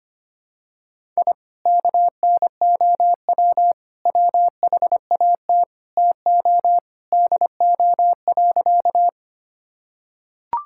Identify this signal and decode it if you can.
Morse code, heard as a keyed tone, about 25 wpm, 700 Hz. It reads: I KNOW WHAT TO DO.
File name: practice / 25wpm / Sentences with 4-Word Phrases - 25wpm